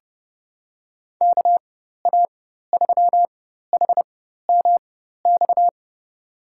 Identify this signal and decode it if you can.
Morse code, heard as a keyed tone, about 31 wpm, spaced as for 14 wpm, 690 Hz. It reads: KA3HMX
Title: USA Call Signs - 4x Character Spacing - 30wpm